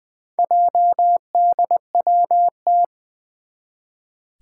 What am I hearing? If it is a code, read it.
Morse code, heard as a keyed tone, about 20 wpm, 700 Hz. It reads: JDWT
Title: Four Challenging Letter-Number Combinations 20wpm